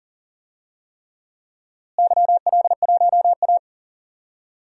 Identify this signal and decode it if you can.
Morse code, heard as a keyed tone, about 40 wpm, 700 Hz. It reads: YL1A
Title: DXCC-100 Call Signs - 4 Characters - Repeated 3x - 40wpm